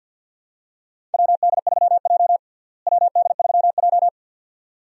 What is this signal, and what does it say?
Morse code, heard as a keyed tone, about 48 wpm, 700 Hz. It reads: WD3J WD3J
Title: USA Call Signs - 4 Characters - Repeated 2x - 50wpm